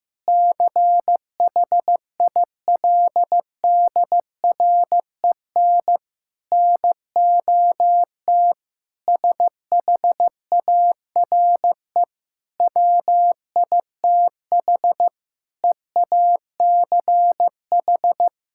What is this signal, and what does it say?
Morse code, heard as a keyed tone, about 15 wpm, 700 Hz. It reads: CHILDREN NOT SHARE WITH EACH
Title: Sentences from Top 1000 Words - Review Part 6 - 15wpm